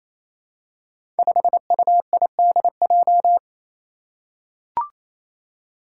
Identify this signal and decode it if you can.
Morse code, heard as a keyed tone, about 27 wpm, 700 Hz. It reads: 5UIDJ
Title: Five Challenging Letter-Number Combinations 28wpm